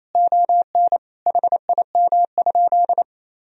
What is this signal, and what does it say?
Morse code, heard as a keyed tone, about 28 wpm, 700 Hz. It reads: ON HIM?
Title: Sentences from Top 1000 Words - Review Part 1 - 28wpm